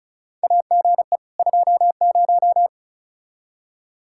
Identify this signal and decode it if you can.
Morse code, heard as a keyed tone, about 36 wpm, 700 Hz. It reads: AGE 20
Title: Practice QSO - Exchange 3 - More Info - 35wpm